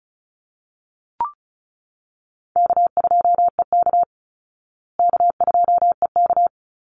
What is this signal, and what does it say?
Morse code, heard as a keyed tone, about 34 wpm, 700 Hz. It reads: K2EK K2EK
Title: Easy Call Signs Repeated 2x - 35wpm